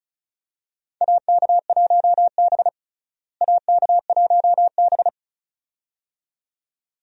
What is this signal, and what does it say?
Morse code, heard as a keyed tone, about 36 wpm, 700 Hz. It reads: AK1B AK1B
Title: USA Call Signs - 4 Characters - Repeated 2x - 35wpm